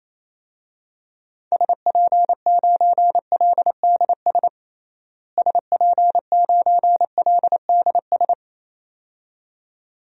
Morse code, heard as a keyed tone, about 28 wpm, 700 Hz. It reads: SP9LDS SP9LDS
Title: DXCC-100 Call Signs - 6 Characters - Repeated 2x - 28wpm